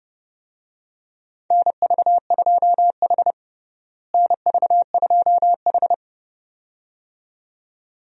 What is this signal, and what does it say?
Morse code, heard as a keyed tone, about 30 wpm, 700 Hz. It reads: NV2H NV2H
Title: USA Call Signs - 4 Characters - Repeated 2x - 30wpm